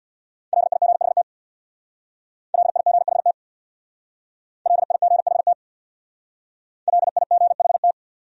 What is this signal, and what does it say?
Morse code, heard as a keyed tone, about 63 wpm, 700 Hz. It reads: LIGHT LIGHT LIGHT LIGHT